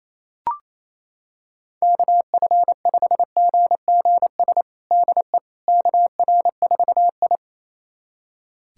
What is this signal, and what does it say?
Morse code, heard as a keyed tone, about 28 wpm, 700 Hz. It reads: KF5GGS DE KR4I
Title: USA Callsign DE Callsign 28wpm